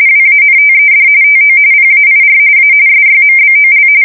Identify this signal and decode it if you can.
RTTY, about 45.45 baud. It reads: WANTED TO PAIR A SYMBOL
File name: fsk170 / USS-Missouri